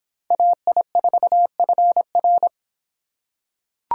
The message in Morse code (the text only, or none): AI4FR